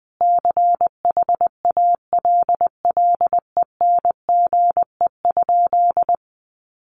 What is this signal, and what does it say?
Morse code, heard as a keyed tone, about 20 wpm, 700 Hz. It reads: CHALLENGE?
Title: Sentences from Top 1300 Words 20wpm